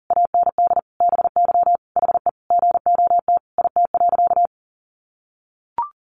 Morse code, heard as a keyed tone, about 40 wpm, 700 Hz. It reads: AND BY HE GOT IT.
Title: Sentences from Top 300 Words 40wpm